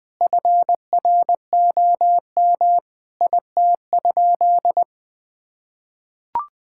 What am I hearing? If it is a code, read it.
Morse code, heard as a keyed tone, about 20 wpm, 700 Hz. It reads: FROM IT?
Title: Sentences from Top 1000 Words - Review Part 5 - 20wpm